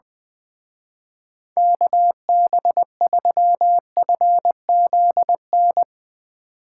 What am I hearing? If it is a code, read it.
Morse code, heard as a keyed tone, about 22 wpm, 700 Hz. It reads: KB3FZN
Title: Speed Racing - USA 2x3 Call Signs 20wpm